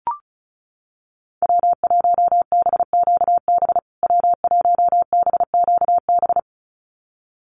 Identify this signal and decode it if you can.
Morse code, heard as a keyed tone, about 34 wpm, 700 Hz. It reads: W1BQB W1BQB
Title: USA Call Signs Repeated 2x 35wpm